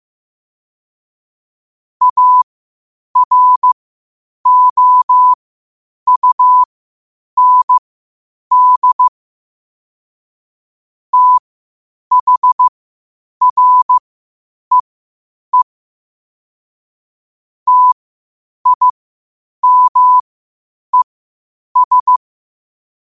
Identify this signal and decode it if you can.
Morse code, heard as a keyed tone, about 15 wpm, spaced as for 8 wpm, 1000 Hz. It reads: AROUND THREE TIMES
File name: qotdfiles / 8wpm